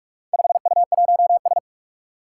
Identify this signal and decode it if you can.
Morse code, heard as a keyed tone, about 44 wpm, 700 Hz. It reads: 5U1S
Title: Four Challenging Letter-Number Combinations 45wpm